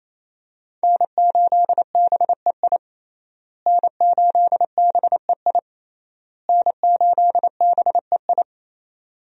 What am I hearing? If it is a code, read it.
Morse code, heard as a keyed tone, about 28 wpm, 700 Hz. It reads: N8BEI N8BEI N8BEI
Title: USA Call Signs - 5 Characters - Repeated 3x - 28wpm